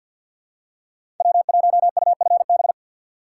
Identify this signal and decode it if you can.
Morse code, heard as a keyed tone, about 53 wpm, 700 Hz. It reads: W1UFB
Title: USA Call Signs - 5 Characters - 3x Character Spacing - 50wpm